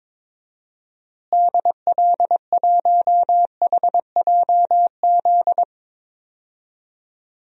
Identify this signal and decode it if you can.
Morse code, heard as a keyed tone, about 22 wpm, 700 Hz. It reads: DL1HJZ
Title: DXCC-100 Call Signs - 6 Characters - Repeated 3x - 22wpm